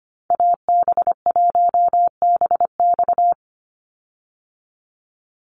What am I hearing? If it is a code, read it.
Morse code, heard as a keyed tone, about 25 wpm, 700 Hz. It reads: AB1BX
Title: Call Signs Repeated 3x - 25wpm